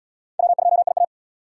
Morse code, heard as a keyed tone, about 72 wpm, 700 Hz. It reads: W2IA